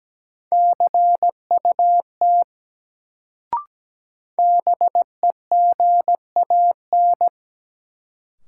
Morse code, heard as a keyed tone, about 17 wpm, 700 Hz. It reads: CUT BEGAN
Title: Sentences from Top 300 Words - New Word Drill - 17wpm